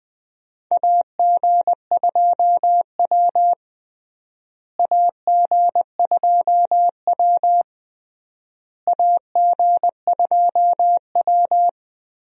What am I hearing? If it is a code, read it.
Morse code, heard as a keyed tone, about 20 wpm, 700 Hz. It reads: AG2W AG2W AG2W
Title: USA Call Signs - 4 Characters - Repeated 3x - 20wpm